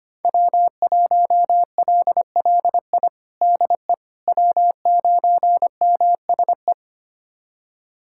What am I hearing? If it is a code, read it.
Morse code, heard as a keyed tone, about 25 wpm, 700 Hz. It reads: W1LLI DE W9MSE
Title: USA Callsign DE Callsign - 5 Characters - 25wpm